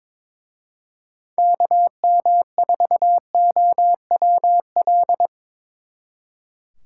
Morse code, heard as a keyed tone, about 24 wpm, 700 Hz. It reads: KM4OWL
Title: Speed Racing - USA Call Signs 22wpm